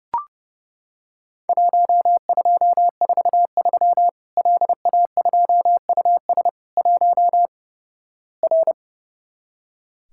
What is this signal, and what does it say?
Morse code, heard as a keyed tone, about 32 wpm, 700 Hz. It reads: 1243 LA2US 1 R
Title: Worked All Europe DX Contest - QTC Exchange - 30wpm